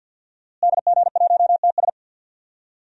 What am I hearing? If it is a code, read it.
Morse code, heard as a keyed tone, about 46 wpm, 700 Hz. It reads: DG1TS
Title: DXCC-100 Call Signs - 5 Characters - Repeated 2x - 50wpm